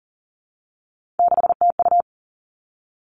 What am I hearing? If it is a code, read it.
Morse code, heard as a keyed tone, about 42 wpm, 700 Hz. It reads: 6TU
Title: Three Challenging Letter-Number Combinations 40wpm